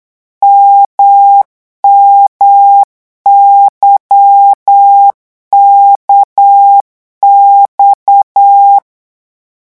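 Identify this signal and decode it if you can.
Morse code, heard as a keyed tone, about 8 wpm, 800 Hz. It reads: MMYKX
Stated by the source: Morse code revision lesson at 8 words per minute, from a French amateur radio course.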